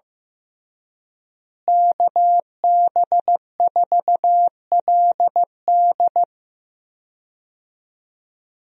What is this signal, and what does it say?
Morse code, heard as a keyed tone, about 14 wpm, 700 Hz. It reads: KB4LD
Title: USA Call Signs - 5 Characters - Repeated 2x - 15wpm